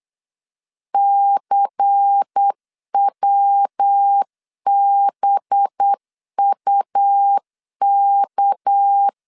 Morse code, heard as a keyed tone, about 8 wpm, 800 Hz. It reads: CWBUK
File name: LEZIONI / 6 PKX0 / 6 PKX 8wpm 3041